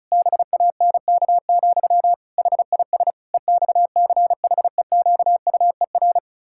Morse code, heard as a keyed tone, about 36 wpm, 700 Hz. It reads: BANK, HIS EXCHEQUER
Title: Pangram Challenge v2 - 35wpm